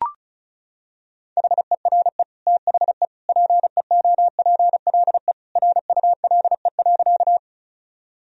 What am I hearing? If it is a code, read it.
Morse code, heard as a keyed tone, about 36 wpm, 700 Hz. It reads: HERE THE PEOPLE RULE.
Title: Sentences from Top 1000 Words - Review Part 2 - 35wpm